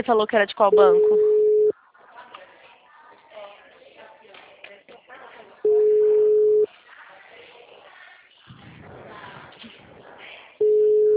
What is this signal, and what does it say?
Ringback tone (Europe, 425 Hz), with speech.